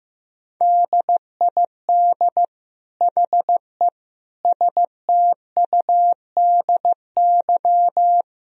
Morse code, heard as a keyed tone, about 15 wpm, 700 Hz. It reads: DID HE STUDY
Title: Sentences from Top 1000 Words - Review Part 1 - 15wpm